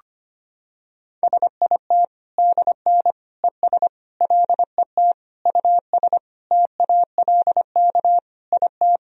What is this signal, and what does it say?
Morse code, heard as a keyed tone, about 25 wpm, 700 Hz. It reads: SIT DN ES LET US TALK IT
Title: Sentences from Top 500 Words and Abbreviations 25wpm